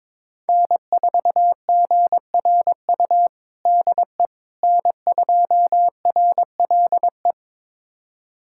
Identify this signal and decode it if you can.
Morse code, heard as a keyed tone, about 21 wpm, 700 Hz. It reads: N4GRU DE N2RLE
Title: USA Callsign DE Callsign - 5 Characters - 22wpm